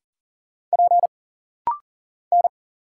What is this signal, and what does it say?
Morse code, heard as a keyed tone, about 40 wpm, 700 Hz. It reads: P N
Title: Single Letter-Number Alt - Rapid-Fire 40wpm